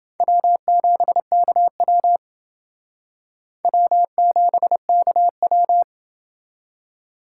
Morse code, heard as a keyed tone, about 29 wpm, 700 Hz. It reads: W7KW W7KW